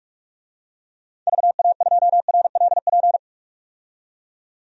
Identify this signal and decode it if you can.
Morse code, heard as a keyed tone, about 46 wpm, 700 Hz. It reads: VA2RLP
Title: DXCC-100 Call Signs - 6 Characters - 45wpm